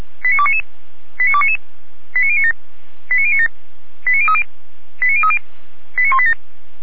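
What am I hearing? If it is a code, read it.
Five-tone selective call (ZVEI), heard as quick runs of five tones, tones of about 70 ms. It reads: 89299 89299 89097 89097 89030 89030 89187
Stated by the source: ZVEI five-tone selective call received on SDR.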